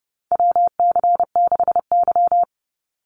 Morse code, heard as a keyed tone, about 30 wpm, 700 Hz. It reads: WC6Y
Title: Call Signs Repeated 2x - 30wpm